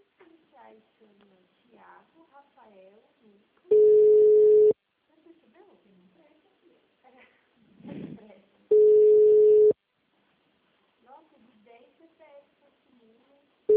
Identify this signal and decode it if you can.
Ringback tone (Europe, 425 Hz), with speech.